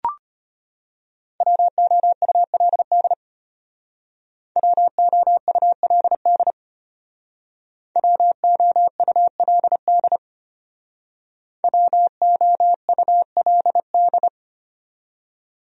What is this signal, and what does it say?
Morse code, heard as a keyed tone, about 33 wpm, 700 Hz. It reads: WOULD WOULD WOULD WOULD